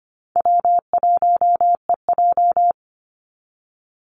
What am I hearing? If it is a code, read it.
Morse code, heard as a keyed tone, about 25 wpm, 700 Hz. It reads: W1EJ
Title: NA SOTA Activator Callsigns 25wpm